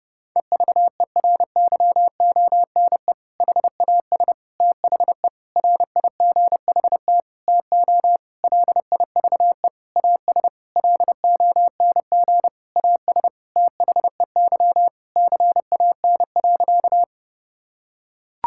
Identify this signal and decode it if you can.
Morse code, heard as a keyed tone, about 30 wpm, 700 Hz. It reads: EVERYONE HAS THE RIGHT TO LIVE AS LONG AS THEY CAN.
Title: Sentences from Top 1000 Words - Review Part 5 - 30wpm